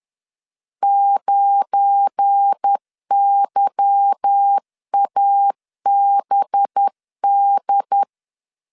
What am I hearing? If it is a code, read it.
Morse code, heard as a keyed tone, about 10 wpm, 800 Hz. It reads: 9YABD
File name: LEZIONI / 8 YJ45 / 8 Y 10wpm 2615